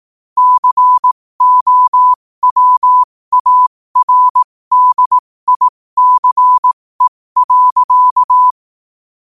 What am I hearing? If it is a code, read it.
Morse code, heard as a keyed tone, about 18 wpm, spaced as for 15 wpm, 1000 Hz. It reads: COWARDICE.
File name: qotdfiles / 15wpm